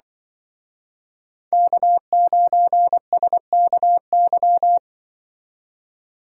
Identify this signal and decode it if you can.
Morse code, heard as a keyed tone, about 23 wpm, 700 Hz. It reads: K9SKY
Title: Speed Racing - USA 1x3 Call Signs 20wpm